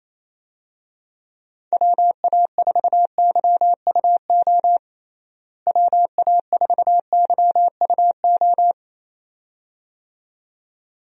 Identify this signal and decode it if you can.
Morse code, heard as a keyed tone, about 28 wpm, 700 Hz. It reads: WA4YUO WA4YUO
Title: USA Call Signs - 6 Characters - Repeated 2x - 28wpm